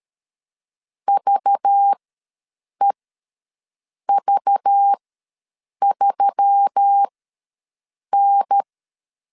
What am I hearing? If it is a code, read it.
Morse code, heard as a keyed tone, about 13 wpm, spaced as for 6 wpm, 800 Hz. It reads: VEV3N
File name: LEZIONI / 5 LBVC / 5 LBV ws 12wpm 3015